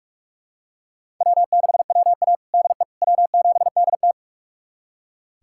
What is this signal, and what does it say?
Morse code, heard as a keyed tone, about 41 wpm, 700 Hz. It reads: W6WA DE W7DT
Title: USA Callsign DE Callsign - 4 Characters - 45wpm